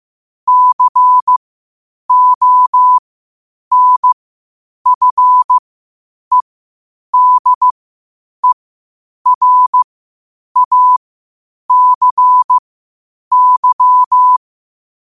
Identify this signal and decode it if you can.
Morse code, heard as a keyed tone, about 15 wpm, spaced as for 8 wpm, 1000 Hz. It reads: CONFEDERACY
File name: qotdfiles / 8wpm